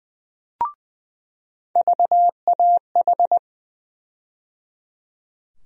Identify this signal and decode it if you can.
Morse code, heard as a keyed tone, about 20 wpm, 700 Hz. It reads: VAH
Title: USA Call Sign Postfixes - 3 Letters 20wpm